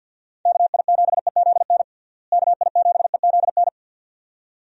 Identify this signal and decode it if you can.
Morse code, heard as a keyed tone, about 53 wpm, 700 Hz. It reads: KI7EZN KI7EZN
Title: USA Call Signs - 6 Characters - Repeated 2x - 50wpm